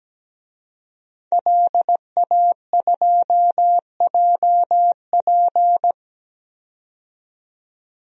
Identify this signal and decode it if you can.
Morse code, heard as a keyed tone, about 17 wpm, 700 Hz. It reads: LA2JP
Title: DXCC-100 Call Signs - 5 Characters - 17wpm